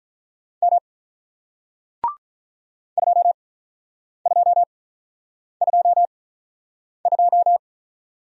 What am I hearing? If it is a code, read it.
Morse code, heard as a keyed tone, about 48 wpm, spaced as for 9 wpm, 700 Hz. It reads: M 2222